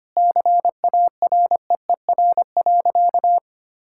Morse code, heard as a keyed tone, about 25 wpm, 700 Hz. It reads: CAREER.
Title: Sentences from Top 1000 Words - Review Part 6 - 25wpm